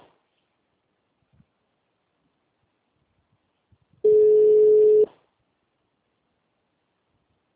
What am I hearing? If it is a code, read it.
Ringback tone (Europe, 425 Hz).
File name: out- -2016- - - .5371